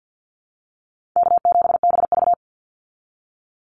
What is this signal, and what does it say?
Morse code, heard as a keyed tone, about 50 wpm, 700 Hz. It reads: K7BV
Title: Call Signs Repeated 3x - 50wpm